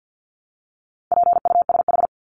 Morse code, heard as a keyed tone, about 50 wpm, 700 Hz. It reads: PUSH
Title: Top 1001-1250 Words 50wpm